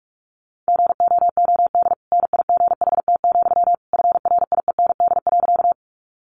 Morse code, heard as a keyed tone, about 44 wpm, 700 Hz. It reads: GOOD NIGHT, FRIEND.
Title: Sentences from Top 300 Words 45wpm